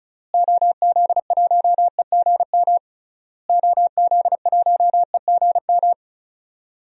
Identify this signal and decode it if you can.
Morse code, heard as a keyed tone, about 34 wpm, 700 Hz. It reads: OZ1EGM OZ1EGM
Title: DXCC-100 Call Signs - 6 Characters - Repeated 3x - 35wpm